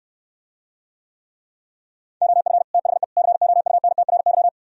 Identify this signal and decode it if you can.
Morse code, heard as a keyed tone, about 69 wpm, 700 Hz. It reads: OF THE COUNTRY